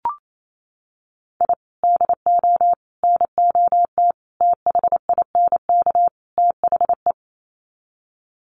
Morse code, heard as a keyed tone, about 28 wpm, 700 Hz. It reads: I DO NOT THINK THE
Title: Sets of 5 Words 28wpm